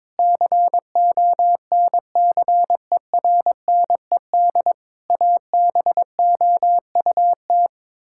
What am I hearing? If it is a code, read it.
Morse code, heard as a keyed tone, about 23 wpm, 700 Hz. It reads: CONCERNED ABOUT
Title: Sentences from Top 1000 Words - Review Part 6 - 22wpm